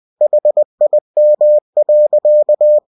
Morse code, heard as a keyed tone, about 20 wpm, 600 Hz. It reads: HIM.